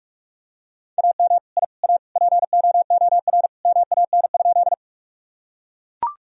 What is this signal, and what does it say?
Morse code, heard as a keyed tone, about 44 wpm, 700 Hz. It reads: AM I A POOR MAN?
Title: Sentences from Top 1000 Words - Review Part 2 - 45wpm